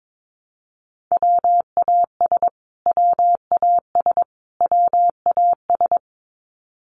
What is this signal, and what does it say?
Morse code, heard as a keyed tone, about 22 wpm, 700 Hz. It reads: WAS WAS WAS
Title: Top 200 Words Repeated 22wpm